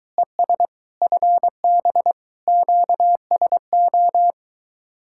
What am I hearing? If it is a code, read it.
Morse code, heard as a keyed tone, about 22 wpm, 700 Hz. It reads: ES FB QSO